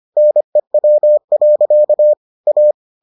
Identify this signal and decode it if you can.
Morse code, heard as a keyed tone, about 25 wpm, 600 Hz. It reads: NEW. A